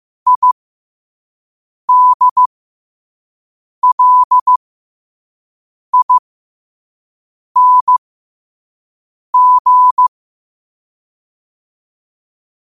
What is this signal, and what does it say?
Morse code, heard as a keyed tone, about 15 wpm, spaced as for 5 wpm, 1000 Hz. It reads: IDLING
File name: qotdfiles / 5wpm